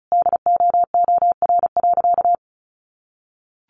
Morse code, heard as a keyed tone, about 34 wpm, 700 Hz. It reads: DOOR.
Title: Sentences from Top 900 Words 35wpm